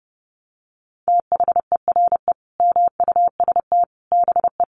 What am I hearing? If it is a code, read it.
Morse code, heard as a keyed tone, about 30 wpm, 700 Hz. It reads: THERE MUST BE
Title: Sentences from Top 700 Words 30wpm